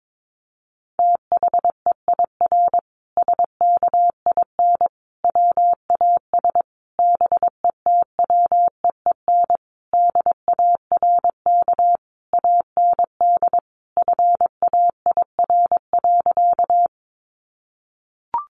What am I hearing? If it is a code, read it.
Morse code, heard as a keyed tone, about 22 wpm, 700 Hz. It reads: THEIR SKIN WAS BETWEEN DARK AND FAIR.
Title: Sentences from Top 1300 Words 22wpm